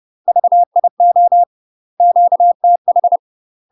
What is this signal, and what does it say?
Morse code, heard as a keyed tone, about 30 wpm, 700 Hz. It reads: VIO QTH
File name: UK MASTER QSOs TRAINING/30WPM